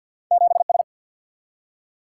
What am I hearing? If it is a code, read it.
Morse code, heard as a keyed tone, about 47 wpm, 700 Hz. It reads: 7S
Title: Single-Letter Single-Number 50wpm